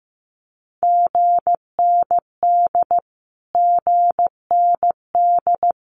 Morse code, heard as a keyed tone, about 15 wpm, 700 Hz. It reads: GND GND
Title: Abbreviations Repeated 3x - 15wpm